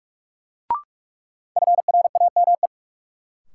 Morse code, heard as a keyed tone, about 47 wpm, 700 Hz. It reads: FRAME
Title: Sentences from Top 1400 Words - New Word Drill Alt - 45wpm